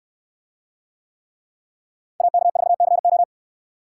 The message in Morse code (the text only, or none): AK4CO